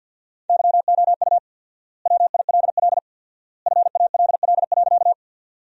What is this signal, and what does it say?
Morse code, heard as a keyed tone, about 53 wpm, 700 Hz. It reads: YOU WILL FALL.
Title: 4x Word-Spacing - Sentences from Top 500 Words - 50wpm